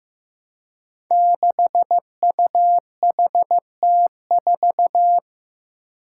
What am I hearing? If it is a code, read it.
Morse code, heard as a keyed tone, about 15 wpm, 700 Hz. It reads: 6UHT4
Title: Five Challenging Letter-Number Combinations 15wpm